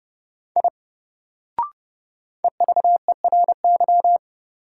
Morse code, heard as a keyed tone, about 31 wpm, 700 Hz. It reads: I EVERY
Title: Sentences from Top 200 Words - New Word Drill - 30wpm